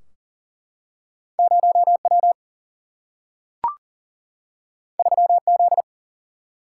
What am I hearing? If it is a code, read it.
Morse code, heard as a keyed tone, about 41 wpm, 700 Hz. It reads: 0W 3Z
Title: Single-Letter Single-Number 40wpm